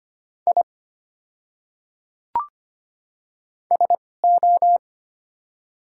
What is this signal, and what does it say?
Morse code, heard as a keyed tone, about 25 wpm, spaced as for 18 wpm, 700 Hz. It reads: I SO
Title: 2x Character Spacing - Top 300 Words - 25wpm